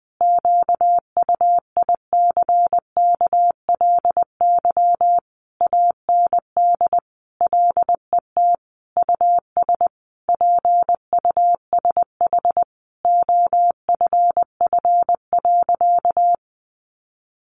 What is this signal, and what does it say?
Morse code, heard as a keyed tone, about 20 wpm, 700 Hz. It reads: QUICKLY AND LET US PUSH OFF.
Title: Sentences from Top 1300 Words 20wpm